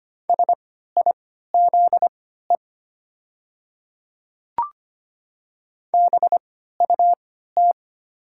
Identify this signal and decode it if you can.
Morse code, heard as a keyed tone, about 25 wpm, spaced as for 14 wpm, 700 Hz. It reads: SIZE BUT
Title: 3x Character Spacing - Top 500 Words - 25wpm